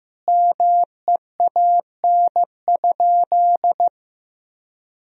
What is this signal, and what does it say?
Morse code, heard as a keyed tone, about 16 wpm, 700 Hz. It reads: MEAN?
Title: Sentences from Top 1000 Words - Review Part 1 - 15wpm